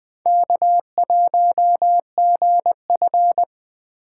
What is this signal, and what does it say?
Morse code, heard as a keyed tone, about 21 wpm, 700 Hz. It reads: K1GF